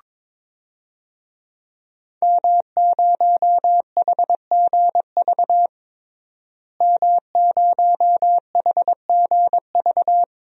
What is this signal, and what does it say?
Morse code, heard as a keyed tone, about 22 wpm, 700 Hz. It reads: M0HGV M0HGV
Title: DXCC-100 Call Signs - 5 Characters - Repeated 3x - 22wpm